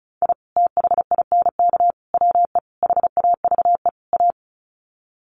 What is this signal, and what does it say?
Morse code, heard as a keyed tone, about 34 wpm, 700 Hz. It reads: I THINK WE HAVE A
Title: Sets of 5 Words Encore 35wpm